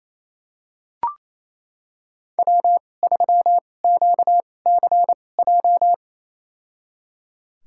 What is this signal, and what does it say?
Morse code, heard as a keyed tone, about 28 wpm, spaced as for 20 wpm, 700 Hz. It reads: W3QCJ
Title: USA Call Signs - 2x Character Spacing - 28wpm